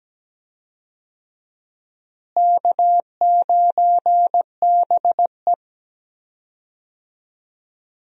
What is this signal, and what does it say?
Morse code, heard as a keyed tone, about 17 wpm, 700 Hz. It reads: K9BE
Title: USA Call Signs - 4 Characters - Repeated 3x - 17wpm